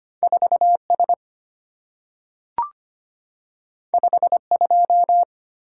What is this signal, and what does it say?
Morse code, heard as a keyed tone, about 25 wpm, 700 Hz. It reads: 4S 52
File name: Two Letter-Number Combinations - 25wpm